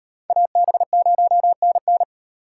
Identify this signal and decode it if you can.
Morse code, heard as a keyed tone, about 38 wpm, 700 Hz. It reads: AB0NN